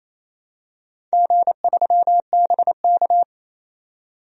Morse code, heard as a keyed tone, about 28 wpm, 700 Hz. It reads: G3BK
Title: DXCC-100 Call Signs - 4 Characters - 28wpm